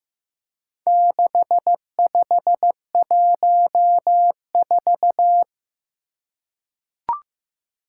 Morse code, heard as a keyed tone, about 15 wpm, 700 Hz. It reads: 6514
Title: Four Challenging Letter-Number Combinations 15wpm